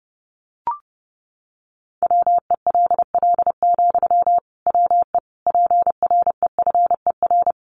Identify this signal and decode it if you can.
Morse code, heard as a keyed tone, about 30 wpm, 700 Hz. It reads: WELL, WE PREFER
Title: Sentences from Top 1100 Words 30wpm